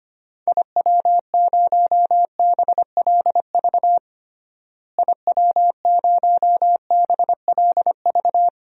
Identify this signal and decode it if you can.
Morse code, heard as a keyed tone, about 25 wpm, 700 Hz. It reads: IW0BLV IW0BLV
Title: DXCC-100 Call Signs - 6 Characters - Repeated 2x - 25wpm